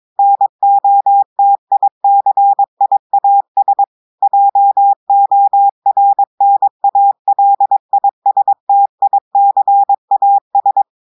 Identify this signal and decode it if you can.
Morse code, heard as a keyed tone, about 22 wpm, 800 Hz. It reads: NOTICIAS JORNALISTICAS
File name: audios / portugais / portugais texte en clair 2 vitesse 22 mots minutes